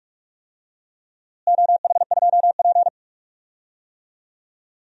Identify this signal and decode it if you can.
Morse code, heard as a keyed tone, about 44 wpm, 700 Hz. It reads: OH2P
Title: DXCC-100 Call Signs - 4 Characters - Repeated 3x - 45wpm